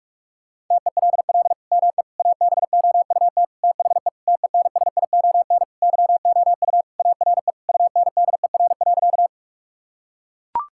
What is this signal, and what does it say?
Morse code, heard as a keyed tone, about 41 wpm, 700 Hz. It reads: TELL ME ABOUT THE TENSION YOU ARE UNDER.
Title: Sentences from Top 1600 Words 45wpm